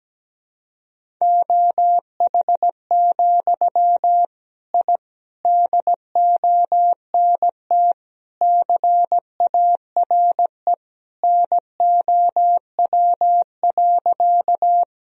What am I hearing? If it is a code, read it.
Morse code, heard as a keyed tone, about 17 wpm, 700 Hz. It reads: OH, I DONT CARE NOW.
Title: Sentences from Top 1000 Words - Review Part 1 - 17wpm